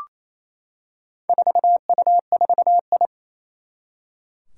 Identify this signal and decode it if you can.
Morse code, heard as a keyed tone, about 28 wpm, 700 Hz. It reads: 4U4I
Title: Four Challenging Letter-Number Combinations 28wpm